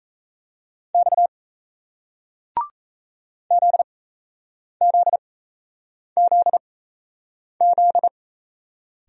Morse code, heard as a keyed tone, about 38 wpm, spaced as for 8 wpm, 700 Hz. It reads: X ZZZZ